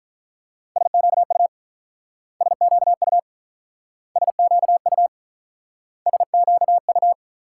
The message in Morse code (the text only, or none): SQU SQU SQU SQU